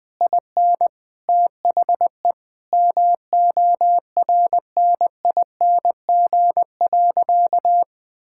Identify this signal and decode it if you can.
Morse code, heard as a keyed tone, about 20 wpm, 700 Hz. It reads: IN THE MORNING.
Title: Sentences from Top 1000 Words - Review Part 5 - 20wpm